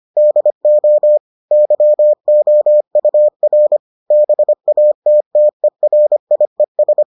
Morse code, heard as a keyed tone, about 25 wpm, 600 Hz. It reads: DO YOUR BATTERIES